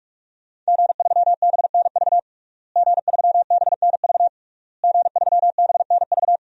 Morse code, heard as a keyed tone, about 46 wpm, 700 Hz. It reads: G3BNV G3BNV G3BNV
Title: DXCC-100 Call Signs - 5 Characters - Repeated 3x - 45wpm